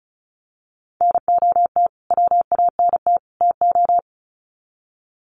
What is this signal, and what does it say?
Morse code, heard as a keyed tone, about 34 wpm, 700 Hz. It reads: NOT WANT TO
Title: Sets of 3 Words 35wpm